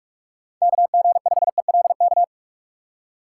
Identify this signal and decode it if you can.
Morse code, heard as a keyed tone, about 42 wpm, 700 Hz. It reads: KG5ELK
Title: USA Call Signs - 6 Characters - 4x Character Spacing - 45wpm